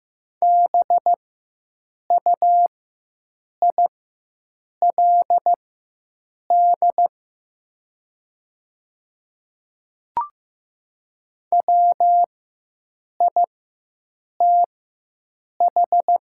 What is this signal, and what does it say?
Morse code, heard as a keyed tone, about 15 wpm, spaced as for 7 wpm, 700 Hz. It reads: BUILD WITH
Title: 4x Character Spacing - Top 300 Words - 15wpm